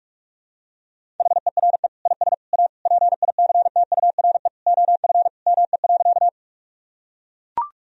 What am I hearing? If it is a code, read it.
Morse code, heard as a keyed tone, about 41 wpm, 700 Hz. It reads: HERE IS A PICTURE OF ME.
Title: Sentences from Top 1000 Words - Review Part 1 - 45wpm